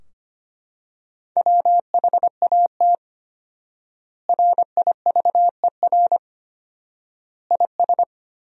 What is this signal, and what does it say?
Morse code, heard as a keyed tone, about 25 wpm, 700 Hz. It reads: WHAT RIVER IS
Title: 4x Word-Spacing - Sentences from Top 300 Words - 25wpm